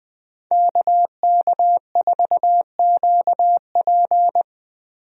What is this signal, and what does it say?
Morse code, heard as a keyed tone, about 20 wpm, 700 Hz. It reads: KK4QP
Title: USA Call Signs - 5 Characters - Repeated 3x - 20wpm